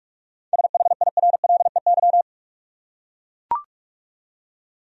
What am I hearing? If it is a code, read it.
Morse code, heard as a keyed tone, about 44 wpm, 700 Hz. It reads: SHIRLEY
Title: USA Common Names - 45wpm